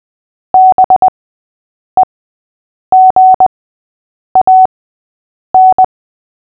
Morse code, heard as a keyed tone, about 20 wpm, spaced as for 8 wpm, 750 Hz. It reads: BEGAN
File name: cw / fists / Short-Story-6-8wpm